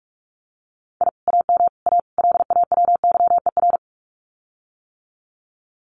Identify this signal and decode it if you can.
Morse code, heard as a keyed tone, about 46 wpm, 700 Hz. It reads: I AM A LAWYER
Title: Common QSO Phrases Encore 45wpm